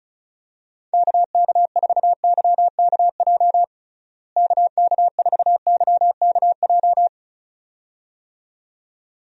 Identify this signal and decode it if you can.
Morse code, heard as a keyed tone, about 36 wpm, 700 Hz. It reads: KK4YKJ KK4YKJ
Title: USA Call Signs - 6 Characters - Repeated 2x - 35wpm